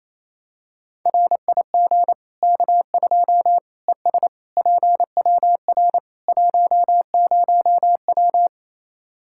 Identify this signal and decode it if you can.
Morse code, heard as a keyed tone, about 28 wpm, 700 Hz. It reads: RIG K2 ES PWR 10W
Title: Practice QSO - Exchange 2 - Info - 28wpm